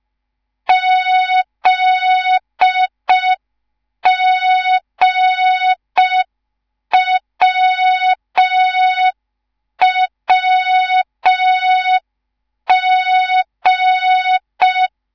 Morse code, heard as a keyed tone, about 5 wpm, 3700 Hz. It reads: ZGWWG